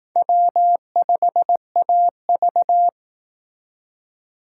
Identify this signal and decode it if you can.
Morse code, heard as a keyed tone, about 19 wpm, 700 Hz. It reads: W5AV
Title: Speed Racing - USA 1x2 Call Signs 15wpm